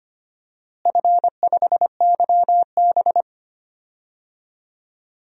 Morse code, heard as a keyed tone, about 25 wpm, 700 Hz. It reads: F5YB
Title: DXCC-100 Call Signs - 4 Characters - Repeated 3x - 25wpm